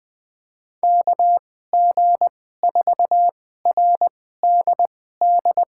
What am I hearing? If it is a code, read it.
Morse code, heard as a keyed tone, about 20 wpm, spaced as for 14 wpm, 700 Hz. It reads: KG4RDD
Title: USA Call Signs - 6 Characters - 2x Character Spacing - 20wpm